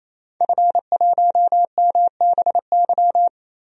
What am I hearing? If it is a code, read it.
Morse code, heard as a keyed tone, about 27 wpm, 700 Hz. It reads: F1MBY
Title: DXCC-100 Call Signs - 5 Characters - Repeated 3x - 28wpm